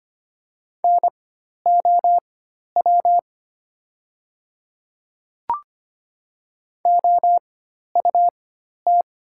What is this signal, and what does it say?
Morse code, heard as a keyed tone, about 25 wpm, spaced as for 12 wpm, 700 Hz. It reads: NOW OUT
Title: 4x Character Spacing - Top 100 Words - 25wpm